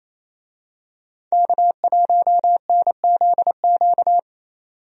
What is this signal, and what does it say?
Morse code, heard as a keyed tone, about 28 wpm, 700 Hz. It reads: K1NZQ
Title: USA Call Signs - 5 Characters - 28wpm